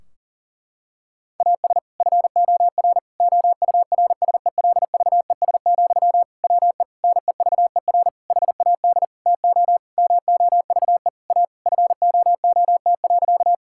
Morse code, heard as a keyed tone, about 37 wpm, 700 Hz. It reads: AS FOR OURSELVES, WE NEVER HAD TO MOVE A FOOT.
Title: Sentences from Top 1600 Words 40wpm